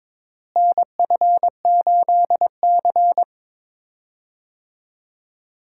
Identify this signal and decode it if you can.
Morse code, heard as a keyed tone, about 22 wpm, 700 Hz. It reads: NF8C
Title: USA Call Signs - 4 Characters - Repeated 3x - 22wpm